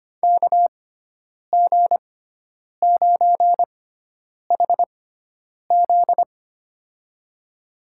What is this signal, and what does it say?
Morse code, heard as a keyed tone, about 25 wpm, spaced as for 9 wpm, 700 Hz. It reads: KG9HZ
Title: USA Call Signs - 6x Character Spacing - 25wpm